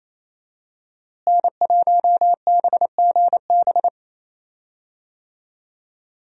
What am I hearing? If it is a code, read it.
Morse code, heard as a keyed tone, about 28 wpm, 700 Hz. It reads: N1BGB
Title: USA Call Signs - 5 Characters - Repeated 2x - 28wpm